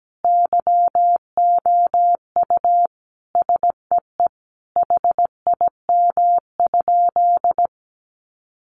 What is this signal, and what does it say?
Morse code, heard as a keyed tone, about 17 wpm, 700 Hz. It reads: YOU SEE HIM?
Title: Sentences from Top 100 Words 17wpm